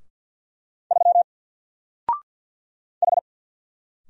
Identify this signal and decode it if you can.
Morse code, heard as a keyed tone, about 50 wpm, 700 Hz. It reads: 3 H
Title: Single Character (Letter:Number:Punctuation:Prosign) 50wpm